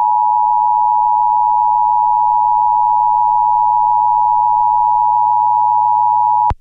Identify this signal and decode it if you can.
Emergency Alert System (SAME) tones (the 853 and 960 Hz attention tone). It is heard from 0.7 to 6.3 s.